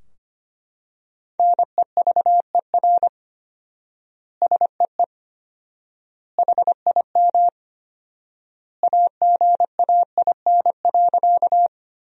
Morse code, heard as a keyed tone, about 25 wpm, 700 Hz. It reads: NEVER SEE HIM AGAIN.
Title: 4x Word-Spacing - Sentences from Top 300 Words - 25wpm